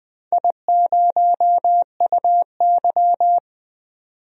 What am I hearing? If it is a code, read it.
Morse code, heard as a keyed tone, about 20 wpm, 700 Hz. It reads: I0UY
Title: DXCC-100 Call Signs - 4 Characters - 20wpm